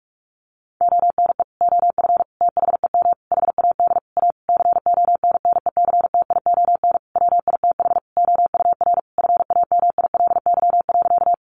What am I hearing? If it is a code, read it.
Morse code, heard as a keyed tone, about 44 wpm, 700 Hz. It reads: ONE OF THEM HAD A CONNECTION WITH OUR FAMILY.
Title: Sentences from Top 1300 Words 45wpm